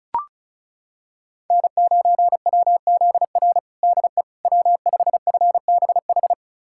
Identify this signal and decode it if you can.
Morse code, heard as a keyed tone, about 36 wpm, 700 Hz. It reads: N9WZR DE W5FBH
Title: USA Callsign DE Callsign - 5 Characters - 35wpm